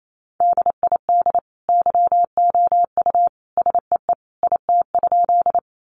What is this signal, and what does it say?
Morse code, heard as a keyed tone, about 28 wpm, 700 Hz. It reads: DID YOU SEE IT?
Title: Sentences from Top 100 Words 28wpm